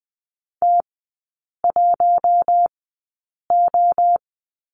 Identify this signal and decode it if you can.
Morse code, heard as a keyed tone, about 20 wpm, spaced as for 8 wpm, 700 Hz. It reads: T1O
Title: 041 Lesson 8e - Number 1 - Three Characters 20wpm